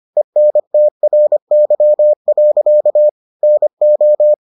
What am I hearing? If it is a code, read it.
Morse code, heard as a keyed tone, about 25 wpm, 600 Hz. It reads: ENTRY. NO